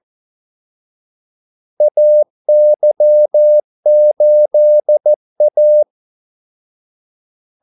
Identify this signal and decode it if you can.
Morse code, heard as a keyed tone, about 14 wpm, 600 Hz. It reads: AY8A
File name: Callsigns-14wpm